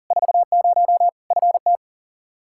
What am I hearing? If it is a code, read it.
Morse code, heard as a keyed tone, about 40 wpm, 700 Hz. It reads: 40 FT
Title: Practice QSO - Exchange 2 - Info - 40wpm